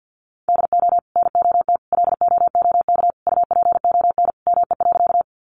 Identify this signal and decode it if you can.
Morse code, heard as a keyed tone, about 50 wpm, 700 Hz. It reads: DO NOT LOOK UPON ME.
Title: Sentences from Top 1100 Words 50wpm